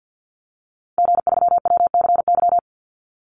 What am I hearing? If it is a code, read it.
Morse code, heard as a keyed tone, about 50 wpm, 700 Hz. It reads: G3WCY
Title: Call Signs Repeated 3x - 50wpm